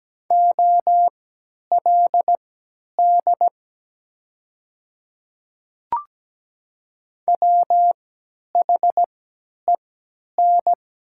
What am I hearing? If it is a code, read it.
Morse code, heard as a keyed tone, about 17 wpm, spaced as for 10 wpm, 700 Hz. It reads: OLD WHEN
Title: 3x Character Spacing - Top 200 Words - 17wpm